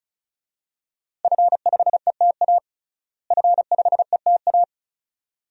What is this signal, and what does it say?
Morse code, heard as a keyed tone, about 34 wpm, 700 Hz. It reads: F5ETA F5ETA
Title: DXCC-100 Call Signs - 5 Characters - Repeated 2x - 35wpm